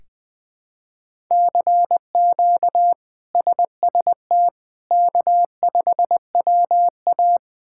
Morse code, heard as a keyed tone, about 20 wpm, 700 Hz. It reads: CQ SST K5WA
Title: K1USN SST QSO Practice - 20wpm